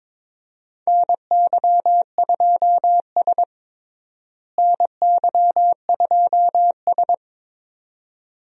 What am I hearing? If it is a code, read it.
Morse code, heard as a keyed tone, about 22 wpm, 700 Hz. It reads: NY2S NY2S
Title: USA Call Signs - 4 Characters - Repeated 2x - 22wpm